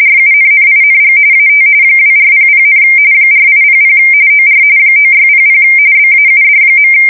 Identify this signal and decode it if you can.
RTTY, about 45.45 baud. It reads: RUNNING AT 2400, 4800 OR 9600 BAUD.
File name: fsk170 / teletype-corp